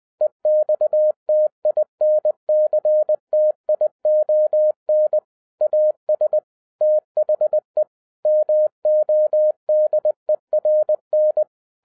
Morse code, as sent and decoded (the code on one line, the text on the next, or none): . -..- - .. -. -.-. - .. --- -. / .- ... / - .... / -- --- -.. .-. -.
EXTINCTION AS THE MODERN